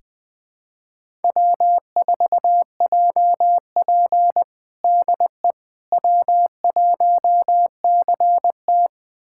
Morse code, heard as a keyed tone, about 20 wpm, 700 Hz. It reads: W4JP DE W1CT
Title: USA Callsign DE Callsign - 4 Characters - 20wpm